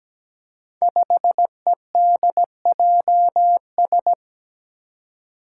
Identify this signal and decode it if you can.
Morse code, heard as a keyed tone, about 17 wpm, 700 Hz. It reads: 5EDJS
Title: Five Challenging Letter-Number Combinations 17wpm